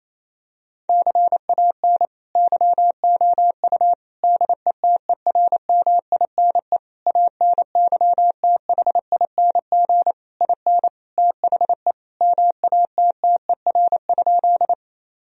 Morse code, heard as a keyed tone, about 28 wpm, 700 Hz. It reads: CAN YOU DETERMINE ANYTHING IN THE MATTER?
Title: Sentences from Top 1000 Words - Review Part 4 - 28wpm